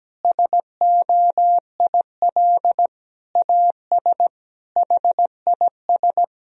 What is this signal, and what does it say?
Morse code, heard as a keyed tone, about 17 wpm, 700 Hz. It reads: SOIL AS HIS
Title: Sentences from Top 1000 Words - Review Part 4 - 17wpm